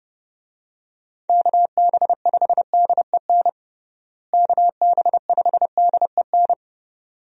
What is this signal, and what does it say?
Morse code, heard as a keyed tone, about 32 wpm, 700 Hz. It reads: KB5DEN KB5DEN
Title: USA Call Signs - 6 Characters - Repeated 2x - 30wpm